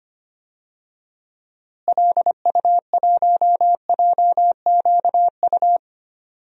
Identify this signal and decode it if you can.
Morse code, heard as a keyed tone, about 25 wpm, 700 Hz. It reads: LU1JQU
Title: DXCC-100 Call Signs - 6 Characters - Repeated 3x - 25wpm